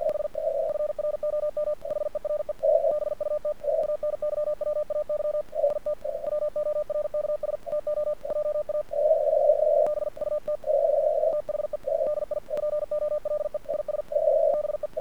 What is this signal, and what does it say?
Morse code, heard as a keyed tone, about 47 wpm, 1200 Hz. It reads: H FROM HERE BUT ANYWAY IT WORKS TO JA BUT THE DIPOLE IS BE
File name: archived / CRSnet